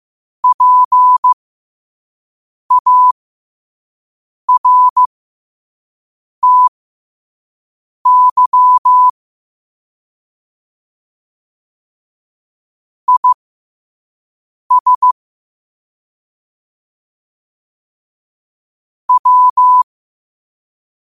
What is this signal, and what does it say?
Morse code, heard as a keyed tone, about 15 wpm, spaced as for 5 wpm, 1000 Hz. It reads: PARTY IS W
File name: qotdfiles / 5wpm